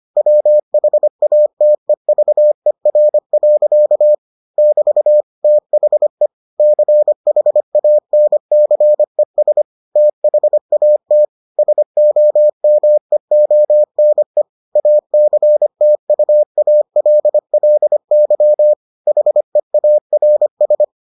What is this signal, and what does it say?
Morse code, heard as a keyed tone, about 25 wpm, 600 Hz. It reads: WHATEVER. <BT> THE CHANCES THAT SOMEONE ACTUALLY HEARS